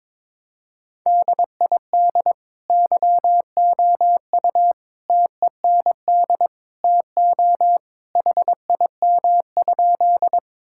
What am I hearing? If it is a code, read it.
Morse code, heard as a keyed tone, about 21 wpm, 700 Hz. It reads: DID YOU TEND TO HIM?
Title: Sentences from Top 1000 Words - Review Part 5 - 22wpm